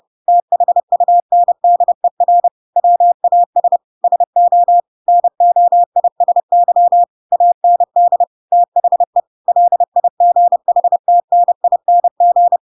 Morse code, heard as a keyed tone, about 30 wpm, 700 Hz. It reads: THUNDER WAS SO NOISY AND THE LIGHTNING
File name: Pinocchio / Pinocchio ch7 30 30 ik1hge